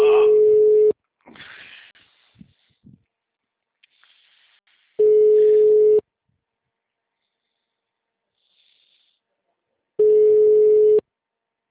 Ringback tone (Europe, 425 Hz).
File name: out- -2007- - - .669